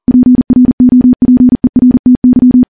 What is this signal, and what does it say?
Morse code, heard as a keyed tone, about 40 wpm, 250 Hz. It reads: PROPERTY
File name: morse-training-40WPM